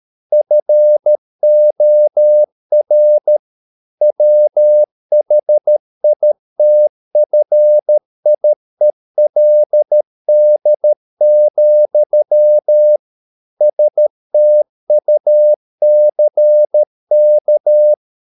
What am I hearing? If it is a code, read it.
Morse code, heard as a keyed tone, about 13 wpm, 600 Hz. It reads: FOR WHITFIELD, STUCK